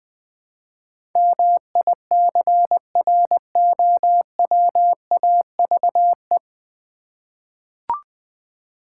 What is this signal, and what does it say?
Morse code, heard as a keyed tone, about 20 wpm, 700 Hz. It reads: MICROWAVE
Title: Easy Words with Hints - 20wpm